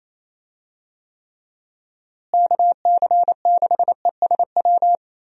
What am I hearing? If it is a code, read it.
Morse code, heard as a keyed tone, about 27 wpm, 700 Hz. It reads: KC6ESW
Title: USA Call Signs - 6 Characters - Repeated 2x - 28wpm